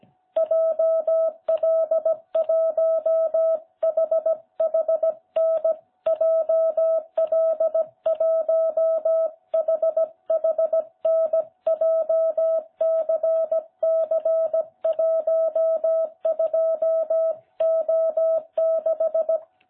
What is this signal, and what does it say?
Morse code, heard as a keyed tone, about 16 wpm, 660 Hz. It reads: JL1HHN JL1HHN JCC12O6